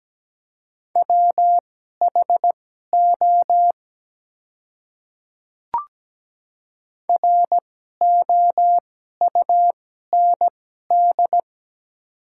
Morse code, heard as a keyed tone, about 17 wpm, spaced as for 12 wpm, 700 Hz. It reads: WHO ROUND
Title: 2x Character Spacing - Top 200 Words - 17wpm